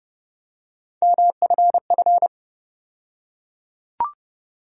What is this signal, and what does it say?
Morse code, heard as a keyed tone, about 31 wpm, 700 Hz. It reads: MFF
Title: USA Call Sign Postfixes - 3 Letters 30wpm